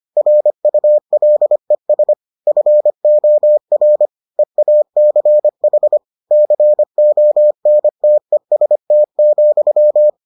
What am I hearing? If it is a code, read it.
Morse code, heard as a keyed tone, about 25 wpm, 600 Hz. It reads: RULES FOR EACH CONTEST,